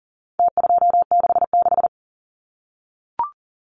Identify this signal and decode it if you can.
Morse code, heard as a keyed tone, about 40 wpm, 700 Hz. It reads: T266
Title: Numbers - Contest Style - 4-Digit 40wpm